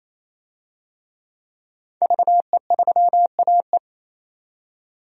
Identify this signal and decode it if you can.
Morse code, heard as a keyed tone, about 28 wpm, 700 Hz. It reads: VE3AE
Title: DXCC-100 Call Signs - 5 Characters - 28wpm